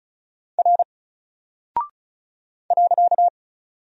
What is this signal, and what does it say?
Morse code, heard as a keyed tone, about 36 wpm, 700 Hz. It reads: R .
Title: Single Character (Letter:Number:Punctuation:Prosign) Alt 35wpm